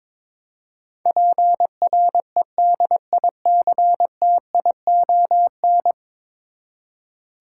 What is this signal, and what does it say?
Morse code, heard as a keyed tone, about 22 wpm, 700 Hz. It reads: PREDICTION
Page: morse-code-ninja-practice-22wpm